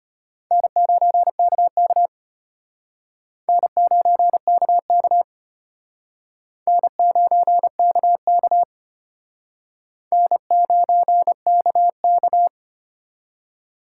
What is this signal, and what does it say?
Morse code, heard as a keyed tone, about 32 wpm, 700 Hz. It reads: N9KK N9KK N9KK N9KK